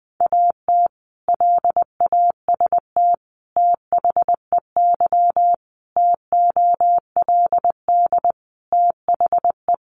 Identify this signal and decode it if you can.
Morse code, heard as a keyed tone, about 20 wpm, 700 Hz. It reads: AT LAST THEY TOLD THE
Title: Sentences from Top 1200 Words 20wpm